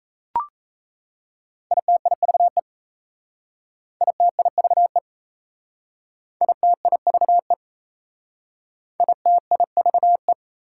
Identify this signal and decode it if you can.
Morse code, heard as a keyed tone, about 35 wpm, 690 Hz. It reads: ITIVE ITIVE ITIVE ITIVE
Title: Speed-Racing Prefixes and Suffixes 28wpm